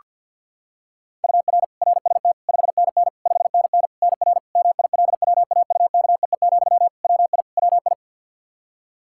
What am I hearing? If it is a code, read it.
Morse code, heard as a keyed tone, about 51 wpm, 700 Hz. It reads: UR RST 5NN 5NN NR MILWAUKEE, WI WI
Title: Practice QSO - Exchange 1 - Beginning - 50wpm